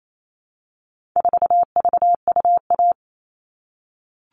Morse code, heard as a keyed tone, about 28 wpm, 700 Hz. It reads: 4VUA
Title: All Cut Numbers - 4-Digit 28wpm